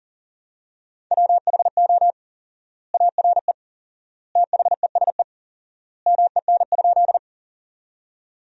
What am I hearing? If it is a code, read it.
Morse code, heard as a keyed tone, about 40 wpm, 700 Hz. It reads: WHO ARE THESE MEN?
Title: 4x Word-Spacing - Sentences from Top 200 Words - 40wpm